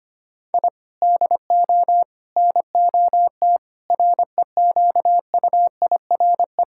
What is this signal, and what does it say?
Morse code, heard as a keyed tone, about 25 wpm, 700 Hz. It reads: I DO NOT REQUIRE
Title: Sentences from Top 1000 Words - Review Part 6 - 25wpm